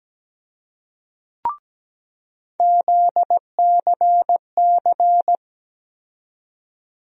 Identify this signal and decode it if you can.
Morse code, heard as a keyed tone, about 17 wpm, 700 Hz. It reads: ZCC
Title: USA Call Sign Postfixes - 3 Letters 17wpm